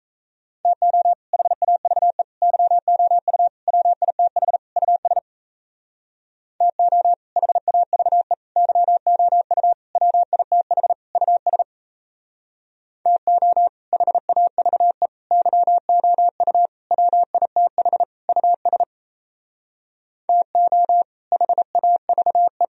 Morse code, heard as a keyed tone, about 36 wpm, 700 Hz. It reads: TO HAVE YOU WITH US TO HAVE YOU WITH US TO HAVE YOU WITH US TO HAVE